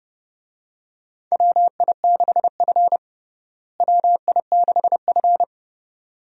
Morse code, heard as a keyed tone, about 29 wpm, 700 Hz. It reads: WI6F WI6F
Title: USA Call Signs - 4 Characters - Repeated 2x - 30wpm